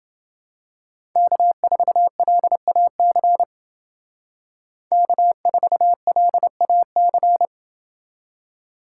Morse code, heard as a keyed tone, about 29 wpm, 700 Hz. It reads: K4LAC K4LAC